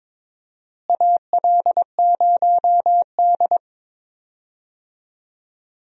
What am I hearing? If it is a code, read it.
Morse code, heard as a keyed tone, about 22 wpm, 700 Hz. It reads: AL0D
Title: USA Call Signs - 4 Characters - Repeated 2x - 22wpm